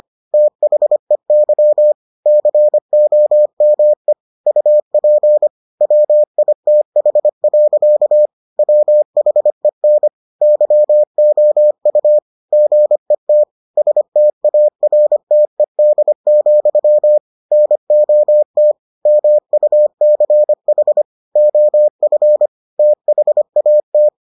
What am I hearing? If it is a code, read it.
Morse code, heard as a keyed tone, about 25 wpm, 600 Hz. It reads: THEY COME UP WITH. WHEN YOU GET STARTED, NOT MUCH OF THAT